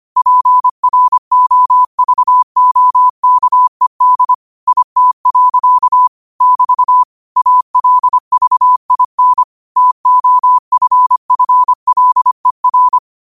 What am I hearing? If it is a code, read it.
Morse code, heard as a keyed tone, about 25 wpm, 1000 Hz. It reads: PROVOKED IT. <BT> ALVIN TOFFLER